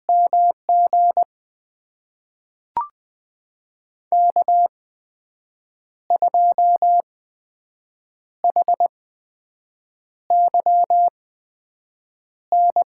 Morse code, heard as a keyed tone, about 20 wpm, spaced as for 6 wpm, 700 Hz. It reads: MG K2HYN
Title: USA Call Signs - 8x Character Spacing - 20wpm